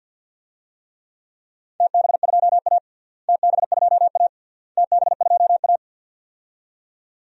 Morse code, heard as a keyed tone, about 53 wpm, 700 Hz. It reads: TB2A TB2A TB2A
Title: DXCC-100 Call Signs - 4 Characters - Repeated 3x - 50wpm